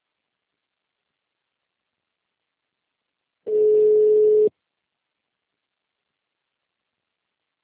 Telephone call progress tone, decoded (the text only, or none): ringback tone